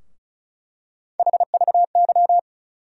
Morse code, heard as a keyed tone, about 36 wpm, 700 Hz. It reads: HVY
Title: USA Call Sign Postfixes - 3 Letters 35wpm